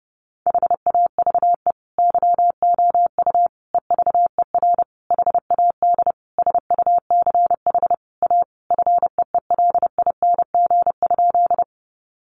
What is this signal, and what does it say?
Morse code, heard as a keyed tone, about 30 wpm, 700 Hz. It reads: HAVE YOU EVER HAD SUCH A FEELING?
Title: Sentences from Top 1200 Words 30wpm